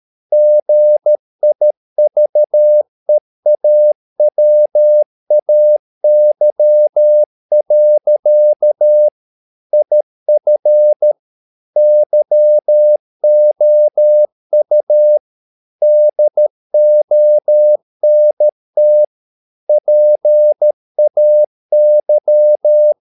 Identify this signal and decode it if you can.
Morse code, heard as a keyed tone, about 13 wpm, 600 Hz. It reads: GIVEAWAY. IF YOU DONT PAY
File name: mp3 files / Lexiphiles - 13wpm